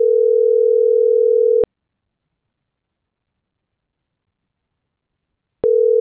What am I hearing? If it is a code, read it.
Ringback tone (North America), 440 + 480 Hz.